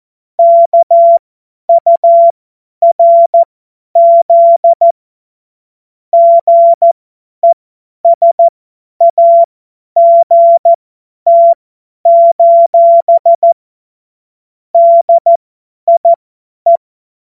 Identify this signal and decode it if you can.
Morse code, heard as a keyed tone, about 14 wpm, spaced as for 10 wpm, 680 Hz. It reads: KURZ GESAGT: DIE
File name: Morsekurs / Witze / Witz 04. Mai 2021, 50 F70